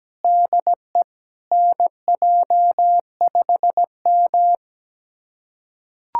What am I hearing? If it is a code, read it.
Morse code, heard as a keyed tone, about 17 wpm, 700 Hz. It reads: DE NJ5M
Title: USA Callsign DE Callsign 17wpm